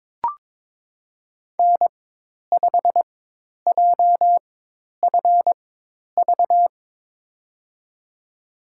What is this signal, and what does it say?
Morse code, heard as a keyed tone, about 21 wpm, spaced as for 10 wpm, 700 Hz. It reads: N5JFV
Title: USA Call Signs - 5 Characters - 4x Character Spacing - 22wpm